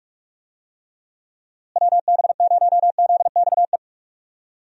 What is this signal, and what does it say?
Morse code, heard as a keyed tone, about 46 wpm, 700 Hz. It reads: WB0ZXE
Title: USA Call Signs - 6 Characters - Repeated 3x - 45wpm